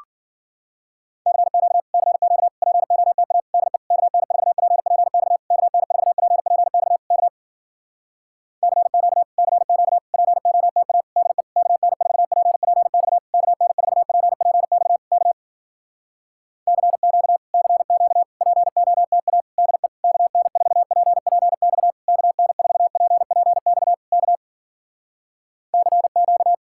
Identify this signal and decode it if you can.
Morse code, heard as a keyed tone, about 53 wpm, 700 Hz. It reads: CQ CQ POTA DE KN4PPX KN4PPX K CQ CQ POTA DE KN4PPX KN4PPX K CQ CQ POTA DE KN4PPX KN4PPX K CQ